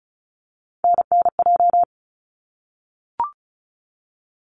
Morse code, heard as a keyed tone, about 34 wpm, 700 Hz. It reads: NNJ
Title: ARRL Sections 35wpm